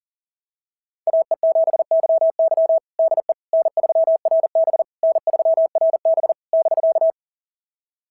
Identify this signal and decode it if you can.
Morse code, heard as a keyed tone, about 44 wpm, 690 Hz. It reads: AE7YY DE N3RB N3RB <BK>
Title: SOTA QSO Practice 40wpm